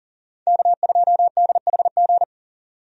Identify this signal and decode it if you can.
Morse code, heard as a keyed tone, about 37 wpm, 700 Hz. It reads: K2DHG
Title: USA Call Signs - 5 Characters - 40wpm